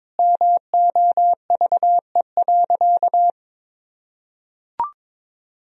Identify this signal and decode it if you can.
Morse code, heard as a keyed tone, about 21 wpm, 700 Hz. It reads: MOVE.
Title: Sentences from Top 1000 Words - Review Part 2 - 22wpm